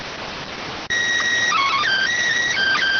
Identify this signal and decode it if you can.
SSTV (slow-scan television), PD 290.
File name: findings / axial iodine 3mm 205.gen-12 rx